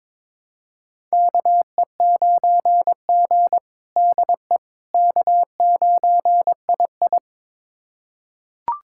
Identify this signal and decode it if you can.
Morse code, heard as a keyed tone, about 22 wpm, 700 Hz. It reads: KE9G DE K9II
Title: USA Callsign DE Callsign - 4 Characters - 22wpm